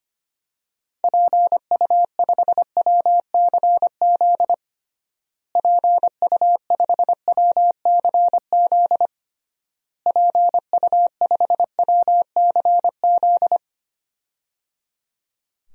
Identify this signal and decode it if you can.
Morse code, heard as a keyed tone, about 25 wpm, 700 Hz. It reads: PU5WCZ PU5WCZ PU5WCZ
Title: DXCC-100 Call Signs - 6 Characters - Repeated 3x - 25wpm